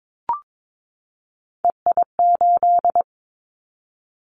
Morse code, heard as a keyed tone, about 22 wpm, 700 Hz. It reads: EI8
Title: Callsign Prefixes for Top 50 DXCC - 22wpm